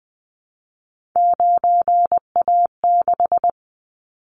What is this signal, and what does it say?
Morse code, heard as a keyed tone, about 19 wpm, 700 Hz. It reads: 9A6
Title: Callsign Prefixes for Top 50 DXCC - 20wpm